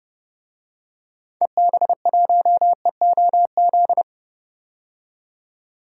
Morse code, heard as a keyed tone, about 31 wpm, 700 Hz. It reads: EB1EOZ
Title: DXCC-100 Call Signs - 6 Characters - 30wpm